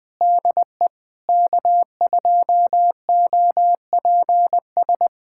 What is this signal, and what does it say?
Morse code, heard as a keyed tone, about 20 wpm, 700 Hz. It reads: DE K2OPS
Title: USA Callsign DE Callsign - 5 Characters - 20wpm